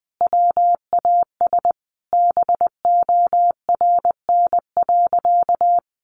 Morse code, heard as a keyed tone, about 20 wpm, 700 Hz. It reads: WAS BORN.
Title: Sentences from Top 1300 Words 20wpm